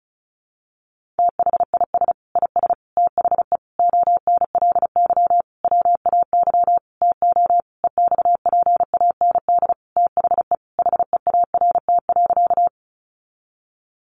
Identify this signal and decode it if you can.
Morse code, heard as a keyed tone, about 34 wpm, 700 Hz. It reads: THIS IS THE ONLY WAY TO EXPAND THE HEART.
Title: Sentences from Top 1400 Words 35wpm